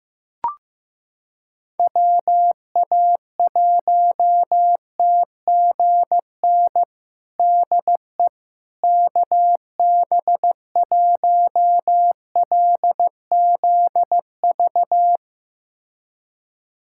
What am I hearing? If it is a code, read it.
Morse code, heard as a keyed tone, about 15 wpm, 700 Hz. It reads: WA1TGN DE KB1LZV
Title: USA Callsign DE Callsign - 6 Characters - 15wpm